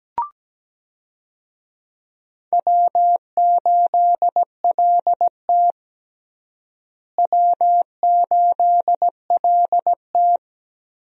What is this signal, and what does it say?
Morse code, heard as a keyed tone, about 17 wpm, 700 Hz. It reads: W8LT W8LT
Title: USA Call Signs - 4 Characters - Repeated 3x - 17wpm